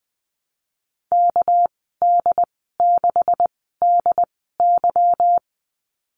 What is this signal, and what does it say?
Morse code, heard as a keyed tone, about 20 wpm, spaced as for 15 wpm, 700 Hz. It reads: KD6DY
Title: 207 Lesson 33g - Letter j - Call Signs 20wpm